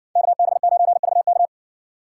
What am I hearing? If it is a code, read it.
Morse code, heard as a keyed tone, about 60 wpm, 700 Hz. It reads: KB9VK